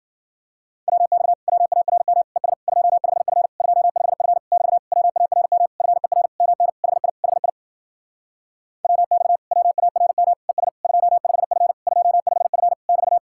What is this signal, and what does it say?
Morse code, heard as a keyed tone, about 52 wpm, 700 Hz. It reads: WX WARM ES 25F 25F <BT> WARM FER MN HI HI WX WARM ES 25F 25F <BT>